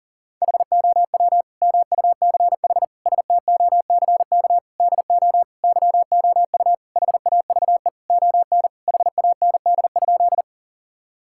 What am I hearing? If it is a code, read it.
Morse code, heard as a keyed tone, about 40 wpm, 700 Hz. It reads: HOW MUCH STOCK DO YOU HAVE ON HAND?
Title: Sentences from Top 1000 Words - Review Part 5 - 40wpm